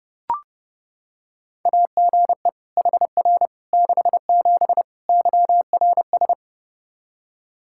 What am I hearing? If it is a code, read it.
Morse code, heard as a keyed tone, about 30 wpm, 700 Hz. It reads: AGE HR 67 YRS
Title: Practice QSO - Exchange 3 - More Info - 30wpm